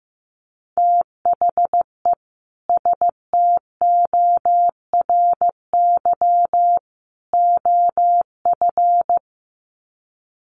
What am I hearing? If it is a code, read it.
Morse code, heard as a keyed tone, about 15 wpm, 700 Hz. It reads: THE STORY OF
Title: Sets of 3 Words 15wpm